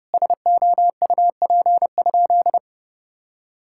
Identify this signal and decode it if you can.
Morse code, heard as a keyed tone, about 30 wpm, 700 Hz. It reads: SOUP?
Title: Sentences from Top 1000 Words - Review Part 6 - 30wpm